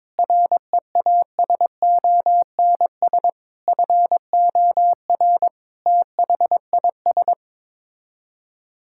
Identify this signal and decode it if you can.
Morse code, heard as a keyed tone, about 22 wpm, 700 Hz. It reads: REASONS FOR THIS
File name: practice / 22wpm / Sets of 8 Words - 22wpm